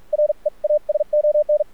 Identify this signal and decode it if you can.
Morse code, heard as a keyed tone, about 46 wpm, spaced as for 38 wpm, 600 Hz. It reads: REASON
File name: archived / CRSnet